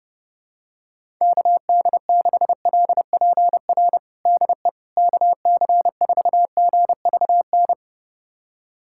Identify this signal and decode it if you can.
Morse code, heard as a keyed tone, about 30 wpm, 700 Hz. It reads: KD6LPR DE KC4GVN
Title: USA Callsign DE Callsign - 6 Characters - 30wpm